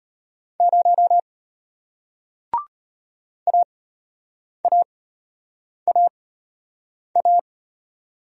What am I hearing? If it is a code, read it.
Morse code, heard as a keyed tone, about 33 wpm, spaced as for 8 wpm, 700 Hz. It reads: 0 AAAA